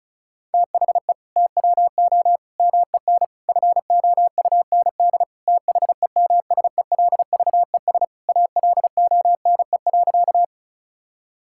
Morse code, heard as a keyed tone, about 36 wpm, 700 Hz. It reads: THE TWO MEN FOUND THEMSELVES ALONE.
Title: Sentences from Top 1000 Words - Review Part 5 - 35wpm